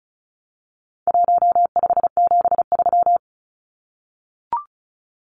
Morse code, heard as a keyed tone, about 34 wpm, 700 Hz. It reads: 1573
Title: Numbers - Contest Style - 4-Digit 35wpm